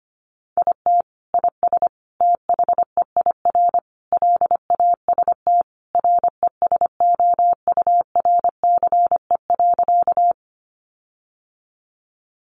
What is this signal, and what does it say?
Morse code, heard as a keyed tone, about 25 wpm, 700 Hz. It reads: IT IS THEIR LAST RESOURCE.
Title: Sentences from Top 1300 Words 25wpm